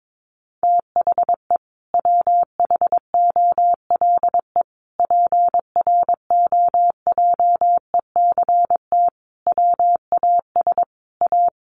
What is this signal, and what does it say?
Morse code, heard as a keyed tone, about 22 wpm, 700 Hz. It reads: THE WHOLE PROJECT WAS A
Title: Sentences from Top 1400 Words 22wpm